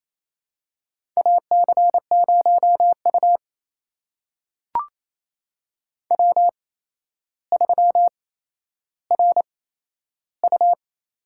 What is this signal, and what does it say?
Morse code, heard as a keyed tone, about 28 wpm, spaced as for 12 wpm, 700 Hz. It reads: AC0U W3RU
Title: USA Call Signs - 8x Character Spacing - 28wpm